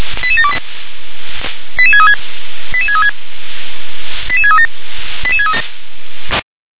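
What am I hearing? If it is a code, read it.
Five-tone selective call (ZVEI), heard as quick runs of five tones, tones of about 70 ms. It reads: 88728 88636 88636 88637
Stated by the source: Five-tone ZVEI selective call, received on SDR.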